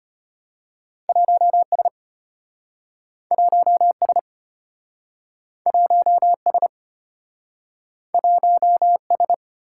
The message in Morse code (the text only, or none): 1S 1S 1S 1S